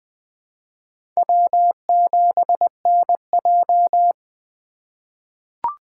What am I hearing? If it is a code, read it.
Morse code, heard as a keyed tone, about 20 wpm, 700 Hz. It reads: W7NJ
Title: USA Call Signs - 4 Characters - 4x Character Spacing - 20wpm